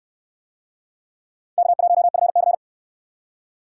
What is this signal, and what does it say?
Morse code, heard as a keyed tone, about 63 wpm, 700 Hz. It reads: K1FC